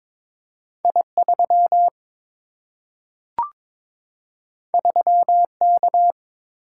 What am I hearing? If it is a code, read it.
Morse code, heard as a keyed tone, about 22 wpm, 700 Hz. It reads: I3 3K
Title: Single-Letter Single-Number 22wpm